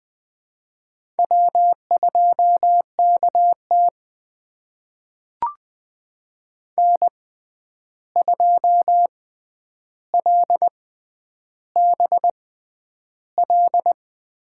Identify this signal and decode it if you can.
Morse code, heard as a keyed tone, about 20 wpm, spaced as for 10 wpm, 700 Hz. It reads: W2KT N2LBL
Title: USA Call Signs - 6x Character Spacing - 20wpm